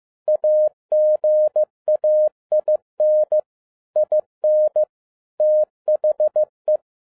Morse code, heard as a keyed tone, about 15 wpm, 610 Hz. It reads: AGAIN IN THE
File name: 2011-10-mp3 / n02-FIELD DAY CONTEST QRV FROM SAKAE VILLAGE 15wpm